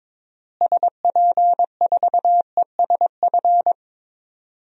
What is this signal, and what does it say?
Morse code, heard as a keyed tone, about 22 wpm, 700 Hz. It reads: SP4ESF
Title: DXCC-100 Call Signs - 6 Characters - Repeated 2x - 22wpm